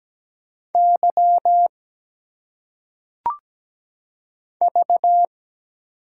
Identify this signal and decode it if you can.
Morse code, heard as a keyed tone, about 17 wpm, 700 Hz. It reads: Y V
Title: Single Letters - Phonetic Pronunciation 17wpm